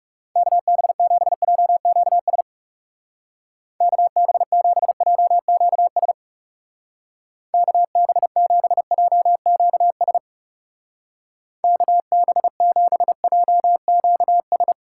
Morse code, heard as a keyed tone, about 36 wpm, 700 Hz. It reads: KB7JQS KB7JQS KB7JQS KB7JQS
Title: Speed Racing - USA Call Signs 30wpm